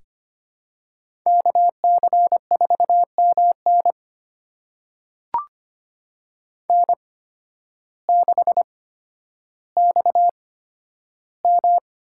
Morse code, heard as a keyed tone, about 25 wpm, spaced as for 12 wpm, 700 Hz. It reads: KC4MN N6XM
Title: USA Call Signs - 8x Character Spacing - 25wpm